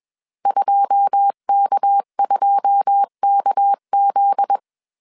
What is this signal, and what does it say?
Morse code, heard as a keyed tone, about 21 wpm, 800 Hz. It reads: 2K2KZ
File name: CW / DET / 1 9 1500C S C 20wpm 2310